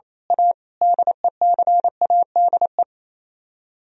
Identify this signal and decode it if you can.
Morse code, heard as a keyed tone, about 27 wpm, 700 Hz. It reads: A DECADE
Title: Speed-Racing - Sets of 5 Words 28wpm